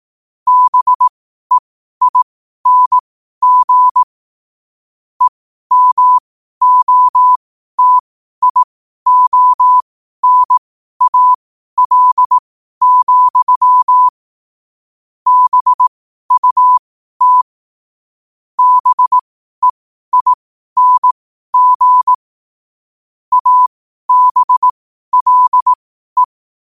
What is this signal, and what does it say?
Morse code, heard as a keyed tone, about 18 wpm, spaced as for 12 wpm, 1000 Hz. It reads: BEING EMOTIONAL, BUT BEING ABLE
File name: qotdfiles / 12wpm